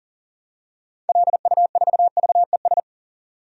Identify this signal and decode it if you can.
Morse code, heard as a keyed tone, about 37 wpm, 700 Hz. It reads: LU4VES
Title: DXCC-100 Call Signs - 6 Characters - 40wpm